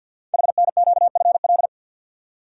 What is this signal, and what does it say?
Morse code, heard as a keyed tone, about 48 wpm, 700 Hz. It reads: HNQFL
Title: Code Groups - 5 Letters - 50wpm